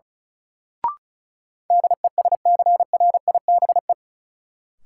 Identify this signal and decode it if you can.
Morse code, heard as a keyed tone, about 36 wpm, 700 Hz. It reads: DESCRIBE
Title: Sentences from Top 1000 Words - New Word Drill - 35wpm